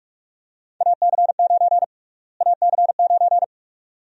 Morse code, heard as a keyed tone, about 47 wpm, 700 Hz. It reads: AC9 AC9
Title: USA Call Sign Prefixes - Repeated 2x - 45wpm